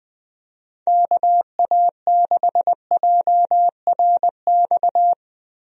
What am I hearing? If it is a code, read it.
Morse code, heard as a keyed tone, about 20 wpm, 700 Hz. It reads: KA6JRX
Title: USA Call Signs - 6 Characters - Repeated 3x - 20wpm